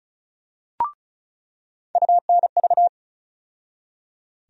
Morse code, heard as a keyed tone, about 34 wpm, 700 Hz. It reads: UNV
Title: USA Call Sign Postfixes - 3 Letters 35wpm